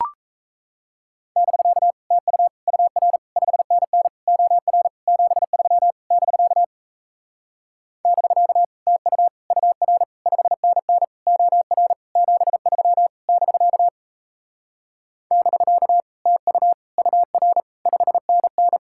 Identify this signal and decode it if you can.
Morse code, heard as a keyed tone, about 37 wpm, 700 Hz. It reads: <BK> TU UR 5NN OR 73 <BK> <BK> TU UR 5NN OR 73 <BK> <BK> TU UR 5NN